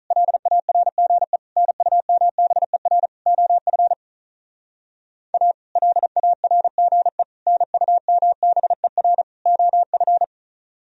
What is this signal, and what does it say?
Morse code, heard as a keyed tone, about 44 wpm, 700 Hz. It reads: LARGE NUMBER OF A LARGE NUMBER OF